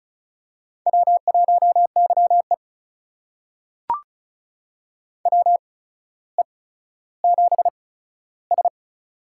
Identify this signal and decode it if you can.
Morse code, heard as a keyed tone, about 36 wpm, spaced as for 15 wpm, 700 Hz. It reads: W1YE WE7S
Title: USA Call Signs - 4 Characters - 8x Character Spacing - 35wpm